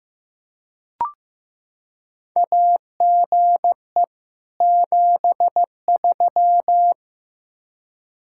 Morse code, heard as a keyed tone, about 15 wpm, 700 Hz. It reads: AGE 73
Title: Practice QSO - Exchange 3 - More Info - 15wpm